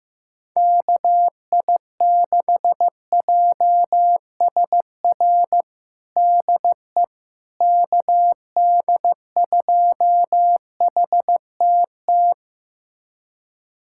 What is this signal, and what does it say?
Morse code, heard as a keyed tone, about 15 wpm, 700 Hz. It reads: KI6JSR DE KD2HTT
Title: USA Callsign DE Callsign - 6 Characters - 15wpm